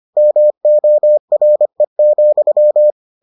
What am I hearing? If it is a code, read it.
Morse code, heard as a keyed tone, about 25 wpm, 600 Hz. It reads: MORE,